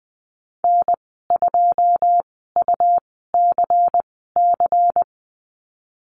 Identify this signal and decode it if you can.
Morse code, heard as a keyed tone, about 20 wpm, spaced as for 14 wpm, 700 Hz. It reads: N2UCC
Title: 154 Lesson 25g - Letter p - Call Signs 20wpm